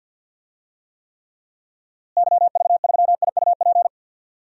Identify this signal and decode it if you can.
Morse code, heard as a keyed tone, about 51 wpm, 700 Hz. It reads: YV3IUP
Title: DXCC-100 Call Signs - 6 Characters - Repeated 3x - 50wpm